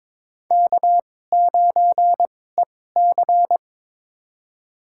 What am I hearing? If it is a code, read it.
Morse code, heard as a keyed tone, about 22 wpm, spaced as for 16 wpm, 700 Hz. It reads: K9EC
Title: USA Call Signs - 2x Character Spacing - 22wpm